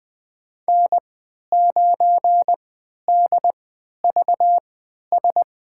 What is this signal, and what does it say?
Morse code, heard as a keyed tone, about 20 wpm, spaced as for 11 wpm, 700 Hz. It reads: N9DVS
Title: USA Call Signs - 5 Characters - 3x Character Spacing - 20wpm